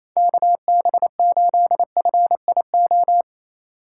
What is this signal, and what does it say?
Morse code, heard as a keyed tone, about 28 wpm, 700 Hz. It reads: KB8FIO
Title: USA Call Signs - 6 Characters - 3x Character Spacing - 28wpm